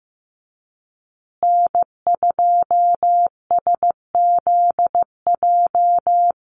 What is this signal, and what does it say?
Morse code, heard as a keyed tone, about 15 wpm, 700 Hz. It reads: N2SZJ
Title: USA Call Signs Repeated 2x 15wpm